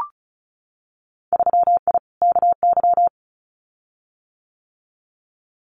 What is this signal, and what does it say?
Morse code, heard as a keyed tone, about 36 wpm, 700 Hz. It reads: 3I KY
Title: Winter Field Day Exchanges 35wpm